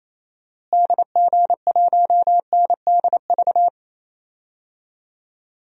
Morse code, heard as a keyed tone, about 28 wpm, 700 Hz. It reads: DG1NDV
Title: DXCC-100 Call Signs - 6 Characters - Repeated 3x - 28wpm